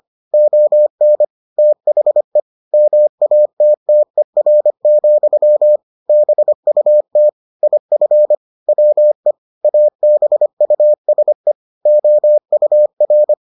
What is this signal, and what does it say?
Morse code, heard as a keyed tone, about 25 wpm, 600 Hz. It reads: ON THE MATTER, BUT IF WE ABUSE OUR